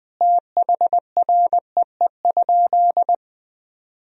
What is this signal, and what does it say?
Morse code, heard as a keyed tone, about 20 wpm, 700 Hz. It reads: THREE?
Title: Sentences from Top 1000 Words - Review Part 2 - 20wpm